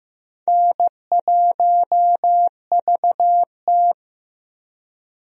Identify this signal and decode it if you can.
Morse code, heard as a keyed tone, about 15 wpm, 700 Hz. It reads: N1VT
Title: USA Call Signs - 4 Characters - Repeated 2x - 15wpm